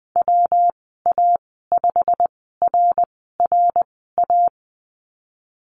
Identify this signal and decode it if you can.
Morse code, heard as a keyed tone, about 20 wpm, spaced as for 15 wpm, 700 Hz. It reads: WA5RRA
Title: 122 Lesson 20g - Number 3 - Call Signs 20wpm